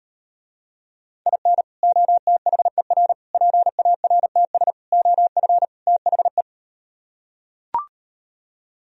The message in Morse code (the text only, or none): IN OTHER PARTS OF THE